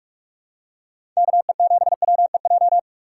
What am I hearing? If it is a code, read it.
Morse code, heard as a keyed tone, about 46 wpm, 700 Hz. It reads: KE7WEJ
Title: USA Call Signs - 6 Characters - 2x Character Spacing - 45wpm